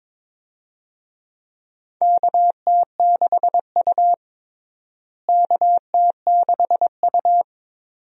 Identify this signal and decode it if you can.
Morse code, heard as a keyed tone, about 22 wpm, 700 Hz. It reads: KT6U KT6U
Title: USA Call Signs - 4 Characters - Repeated 2x - 22wpm